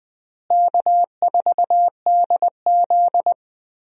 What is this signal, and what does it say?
Morse code, heard as a keyed tone, about 20 wpm, 700 Hz. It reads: K4DZ
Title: USA Call Signs - 4 Characters - Repeated 2x - 20wpm